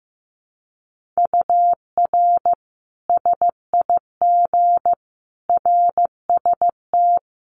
Morse code, heard as a keyed tone, about 15 wpm, 700 Hz. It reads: UR SIG RST
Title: Common QSO Phrases 15wpm